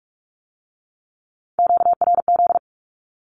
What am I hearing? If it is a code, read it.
Morse code, heard as a keyed tone, about 46 wpm, 700 Hz. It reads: QRZ
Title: Q-Codes with Spoken Meaning - 45wpm